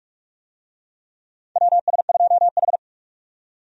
Morse code, heard as a keyed tone, about 46 wpm, 700 Hz. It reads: WS2H
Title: USA Call Signs - 4 Characters - 45wpm